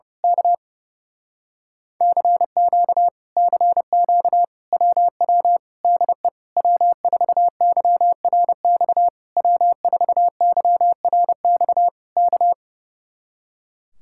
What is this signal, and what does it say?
Morse code, heard as a keyed tone, about 29 wpm, 700 Hz. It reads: K CQ CQ WW DE W4YRX W4YRX K
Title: Speed-Racing Common QSO Phrases - 30wpm